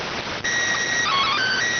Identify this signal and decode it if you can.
SSTV (slow-scan television), PD 290.